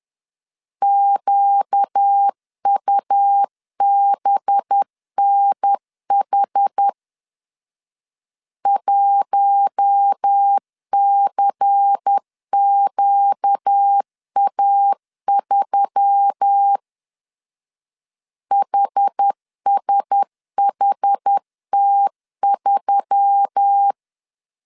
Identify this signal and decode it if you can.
Morse code, heard as a keyed tone, about 10 wpm, 800 Hz. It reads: QUBNH 1CQA3 HSHT3
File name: CW / DET / 1 6 1500C S A 10wpm 4141